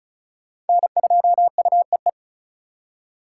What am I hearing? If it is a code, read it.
Morse code, heard as a keyed tone, about 42 wpm, 700 Hz. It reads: N2UEE
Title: Speed Racing - USA 1x3 Call Signs 35wpm